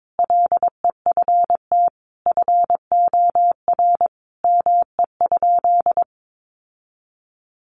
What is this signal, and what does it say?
Morse code, heard as a keyed tone, about 22 wpm, 700 Hz. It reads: LEFT FOR ME?
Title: Sentences from Top 1500 Words 22wpm